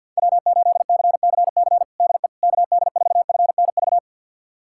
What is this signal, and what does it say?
Morse code, heard as a keyed tone, about 48 wpm, 700 Hz. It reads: W8CCC DE KD4FNV
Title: USA Callsign DE Callsign 50wpm